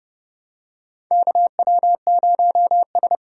Morse code, heard as a keyed tone, about 31 wpm, 700 Hz. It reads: KW0S
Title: USA Call Signs - 4 Characters - 4x Character Spacing - 30wpm